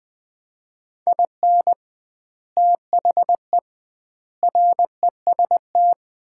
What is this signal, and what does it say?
Morse code, heard as a keyed tone, about 20 wpm, 700 Hz. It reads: IN THE REST
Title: 2x Word-Spacing - Sets of 5 Words - 20wpm